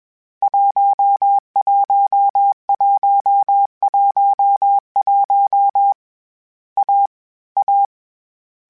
Morse code, heard as a keyed tone, about 21 wpm, 800 Hz. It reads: 11111 A A
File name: Difficult Combos / Difficult Combos-mp3s / AWJ1 @20